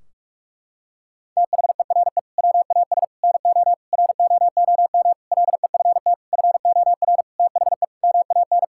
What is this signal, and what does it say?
Morse code, heard as a keyed tone, about 46 wpm, 700 Hz. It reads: THERE WAS NO ROOM LEFT FOR THE MAN
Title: Sentences from Top 1600 Words 45wpm